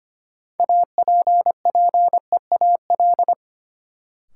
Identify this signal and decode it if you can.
Morse code, heard as a keyed tone, about 25 wpm, 700 Hz. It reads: APPEAL
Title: Sentences from Top 1300 Words - New Word Drill - 25wpm